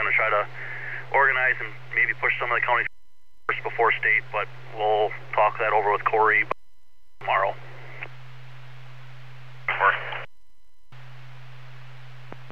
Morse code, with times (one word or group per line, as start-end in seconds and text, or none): none